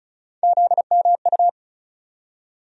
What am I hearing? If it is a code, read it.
Morse code, heard as a keyed tone, about 34 wpm, 700 Hz. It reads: ZMU
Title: USA Call Sign Postfixes - 3 Letters 35wpm